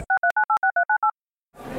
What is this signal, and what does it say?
DTMF, heard as telephone touch tones, tones of about 100 ms. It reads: 63986398